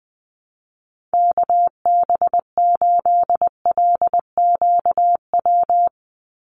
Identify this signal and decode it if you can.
Morse code, heard as a keyed tone, about 20 wpm, 700 Hz. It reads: KB8LQW
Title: USA Call Signs 20wpm